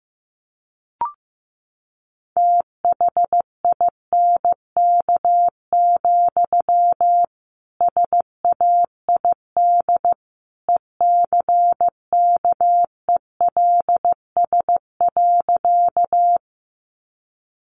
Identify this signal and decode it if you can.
Morse code, heard as a keyed tone, about 15 wpm, 700 Hz. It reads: THINK, SAID ECKELS.
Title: A Sound Of Thunder 15wpm